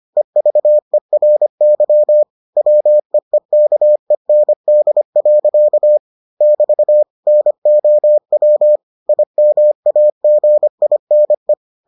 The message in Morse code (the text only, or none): EVERY WEEKEND. <BT> NOW IMAGINE